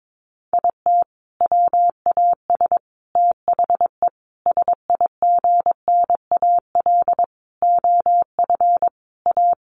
Morse code, heard as a keyed tone, about 22 wpm, 700 Hz. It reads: IT WAS THE SIGNAL OF A
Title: Sentences from Top 1400 Words 22wpm